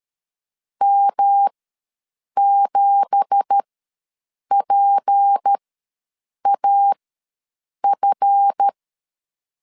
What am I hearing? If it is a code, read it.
Morse code, heard as a keyed tone, about 13 wpm, spaced as for 7 wpm, 800 Hz. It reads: M7PAF
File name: LEZIONI / 9 2867 ws 12wpm 3431